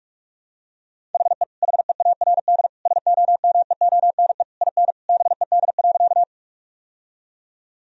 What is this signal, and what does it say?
Morse code, heard as a keyed tone, about 46 wpm, 700 Hz. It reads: HE HEARD SOMEONE IN BED.
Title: Sentences from Top 1000 Words - Review Part 4 - 45wpm